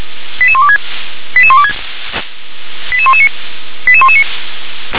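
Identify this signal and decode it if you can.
Five-tone selective call (ZVEI), heard as quick runs of five tones, tones of about 70 ms. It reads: 88126 88126 88119 88119
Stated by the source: ZVEI five-tone selective call received by SDR.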